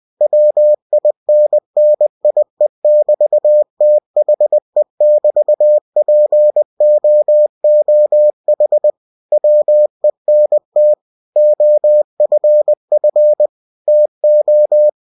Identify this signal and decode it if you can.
Morse code, heard as a keyed tone, about 20 wpm, 600 Hz. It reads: WINNIE<BT>THE<BT>POOH WENT OFF TO